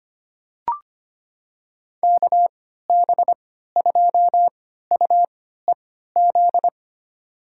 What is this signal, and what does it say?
Morse code, heard as a keyed tone, about 25 wpm, spaced as for 14 wpm, 700 Hz. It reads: KB2UEZ
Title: USA Call Signs - 6 Characters - 3x Character Spacing - 25wpm